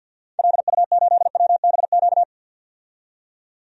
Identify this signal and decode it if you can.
Morse code, heard as a keyed tone, about 52 wpm, 700 Hz. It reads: LU8WBQ
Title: DXCC-100 Call Signs - 6 Characters - 50wpm